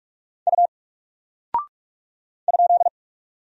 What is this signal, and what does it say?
Morse code, heard as a keyed tone, about 44 wpm, 700 Hz. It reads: U ?
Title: Single Character (Letter:Number:Punctuation:Prosign) 45wpm